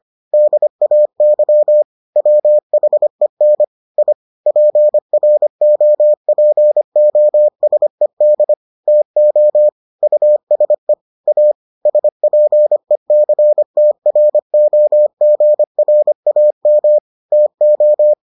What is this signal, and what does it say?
Morse code, heard as a keyed tone, about 25 wpm, 600 Hz. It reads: DAY WHEN I PROPOSED TO USE A SPECTROGRAM TO